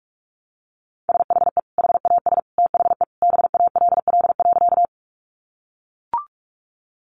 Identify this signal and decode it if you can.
Morse code, heard as a keyed tone, about 44 wpm, 700 Hz. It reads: SHE HAS THE BALL.
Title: Sentences from Top 900 Words 45wpm